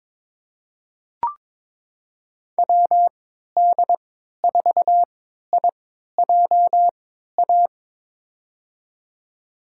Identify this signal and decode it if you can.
Morse code, heard as a keyed tone, about 22 wpm, spaced as for 13 wpm, 700 Hz. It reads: WD4IJA
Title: USA Call Signs - 6 Characters - 3x Character Spacing - 22wpm